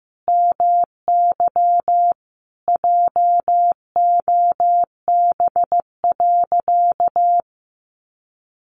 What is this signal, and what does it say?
Morse code, heard as a keyed tone, about 15 wpm, 700 Hz. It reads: MY JOB.
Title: Sentences from Top 600 Words 15wpm